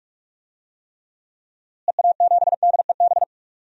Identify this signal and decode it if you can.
Morse code, heard as a keyed tone, about 46 wpm, 700 Hz. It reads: EA7DEB
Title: DXCC-100 Call Signs - 6 Characters - Repeated 2x - 45wpm